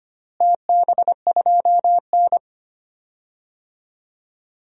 Morse code, heard as a keyed tone, about 25 wpm, 700 Hz. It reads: TB2N
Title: DXCC-100 Call Signs - 4 Characters - Repeated 3x - 25wpm